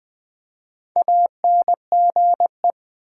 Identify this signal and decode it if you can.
Morse code, heard as a keyed tone, about 20 wpm, 700 Hz. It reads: ANGE
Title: Build A Word - 1 Letter At A Time - 5-Letter Words - 20wpm